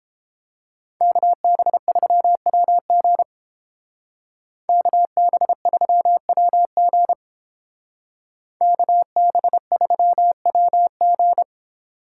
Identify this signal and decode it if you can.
Morse code, heard as a keyed tone, about 31 wpm, 700 Hz. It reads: KB3WG KB3WG KB3WG